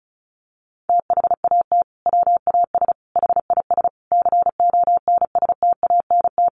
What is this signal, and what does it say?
Morse code, heard as a keyed tone, about 34 wpm, 700 Hz. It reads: THAT WAS HIS CONSTANT
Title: Sentences from Top 1500 Words 35wpm